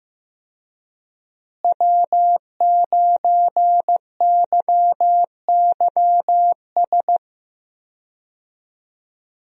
Morse code, heard as a keyed tone, about 15 wpm, 700 Hz. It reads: W9YYS
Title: USA Call Signs - 5 Characters - Repeated 2x - 15wpm